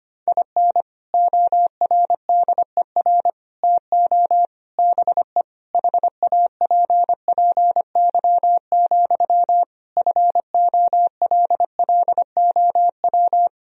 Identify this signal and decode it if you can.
Morse code, heard as a keyed tone, about 25 wpm, 700 Hz. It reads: IN ORDER TO BE HAPPY, FOLLOW
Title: Sentences from Top 1000 Words - Review Part 2 - 25wpm